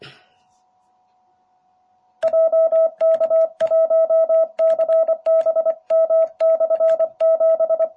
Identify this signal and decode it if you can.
Morse code, heard as a keyed tone, about 25 wpm, 650 Hz. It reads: JK1CBM/7